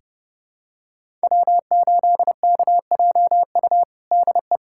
Morse code, heard as a keyed tone, about 30 wpm, 700 Hz. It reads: W8KJU DE
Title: USA Callsign DE Callsign - 5 Characters - 30wpm